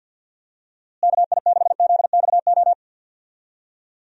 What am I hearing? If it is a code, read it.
Morse code, heard as a keyed tone, about 51 wpm, 700 Hz. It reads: KI6ZXO
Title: USA Call Signs - 6 Characters - Repeated 3x - 50wpm